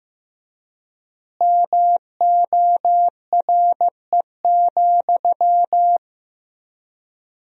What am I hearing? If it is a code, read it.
Morse code, heard as a keyed tone, about 15 wpm, 700 Hz. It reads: MORE,
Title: 3x Word-Spacing - Sentences from Top 300 Words - 15wpm